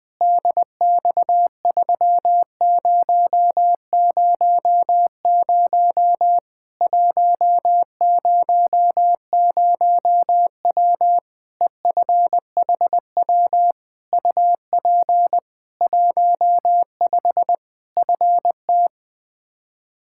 Morse code, heard as a keyed tone, about 20 wpm, 700 Hz. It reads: DX3000 100W EFHW UP 15 FT
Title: Practice QSO - Exchange 2 - Info - 20wpm